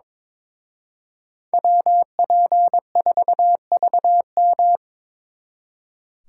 Morse code, heard as a keyed tone, about 26 wpm, 700 Hz. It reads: WP4VM
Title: Speed Racing - USA 2x2 Call Signs 22wpm